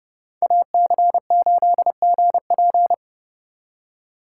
Morse code, heard as a keyed tone, about 35 wpm, 700 Hz. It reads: AC8GP